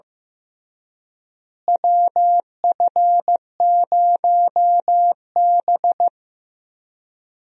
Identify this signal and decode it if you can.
Morse code, heard as a keyed tone, about 16 wpm, 700 Hz. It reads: WF0B
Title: USA Call Signs - 4 Characters - Repeated 3x - 15wpm